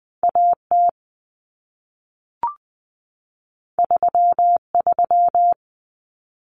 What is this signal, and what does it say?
Morse code, heard as a keyed tone, about 20 wpm, 700 Hz. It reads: AT 33
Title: Common Cut Numbers - 2-Digit 20wpm